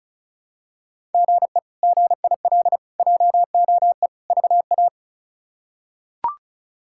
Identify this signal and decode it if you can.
Morse code, heard as a keyed tone, about 36 wpm, 700 Hz. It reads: GE GIL JOE VA
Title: K1USN SST QSO Practice - 35wpm